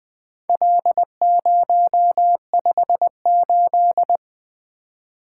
Morse code, heard as a keyed tone, about 22 wpm, 700 Hz. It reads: L058
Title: Speed-Racing - Four Letter-Number Combinations - 20wpm